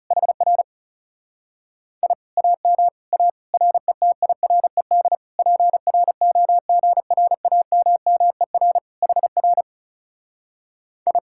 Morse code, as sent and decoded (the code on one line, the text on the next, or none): .... .-. / .. / .- -- / .- / .-. - .. .-. -.. / .--. .-. --- --. .-. .- -- -- . .-. / .... .-. / ..
HR I AM A RETIRED PROGRAMMER HR I